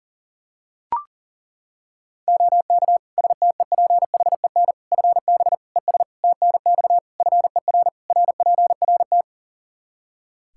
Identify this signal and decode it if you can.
Morse code, heard as a keyed tone, about 40 wpm, 700 Hz. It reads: OK STEPHEN FB ES TNX FER RPRT
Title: Practice QSO - Exchange 2 - Info - 40wpm